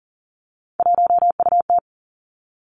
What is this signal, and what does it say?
Morse code, heard as a keyed tone, about 40 wpm, 700 Hz. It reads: 1UT
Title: Three Challenging Letter-Number Combinations 40wpm